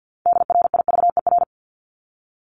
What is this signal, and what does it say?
Morse code, heard as a keyed tone, about 50 wpm, 700 Hz. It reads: DRIVER
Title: Top 751-1000 Words 50wpm